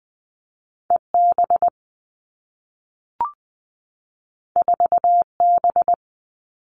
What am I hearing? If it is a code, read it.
Morse code, heard as a keyed tone, about 20 wpm, 700 Hz. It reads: E B 4 B
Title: Two Challenging Letter-Number Combinations 20wpm